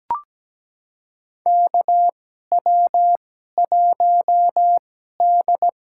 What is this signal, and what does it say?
Morse code, heard as a keyed tone, about 17 wpm, spaced as for 12 wpm, 700 Hz. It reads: KW1D
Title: USA Call Signs - 4 Characters - 2x Character Spacing - 17wpm